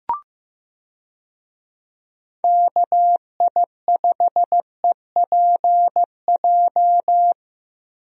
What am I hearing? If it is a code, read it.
Morse code, heard as a keyed tone, about 15 wpm, 700 Hz. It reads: KI5EPJ
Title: USA Call Signs - 6 Characters - Repeated 2x - 15wpm